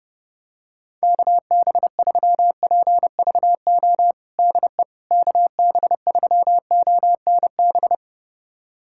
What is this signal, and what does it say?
Morse code, heard as a keyed tone, about 32 wpm, 700 Hz. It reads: KB3PVO DE KB3ONB
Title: USA Callsign DE Callsign - 6 Characters - 30wpm